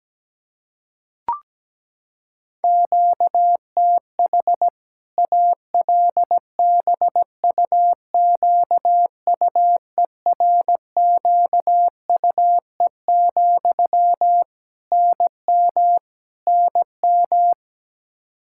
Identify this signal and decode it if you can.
Morse code, heard as a keyed tone, about 17 wpm, 700 Hz. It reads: QTH ALBUQUERQUE, NM NM
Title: K1USN SST QSO Practice 17wpm